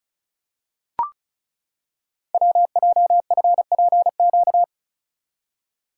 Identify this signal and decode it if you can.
Morse code, heard as a keyed tone, about 36 wpm, 700 Hz. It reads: WJFPQ
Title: Code Groups - 5 Letters - 35wpm